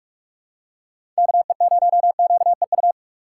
Morse code, heard as a keyed tone, about 43 wpm, 700 Hz. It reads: KE0QEU
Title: USA Call Signs - 3x Character Spacing - 45wpm